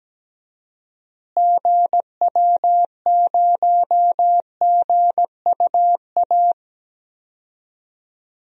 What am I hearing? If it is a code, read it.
Morse code, heard as a keyed tone, about 17 wpm, 700 Hz. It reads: GW0GUA
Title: DXCC-100 Call Signs - 6 Characters - Repeated 2x - 17wpm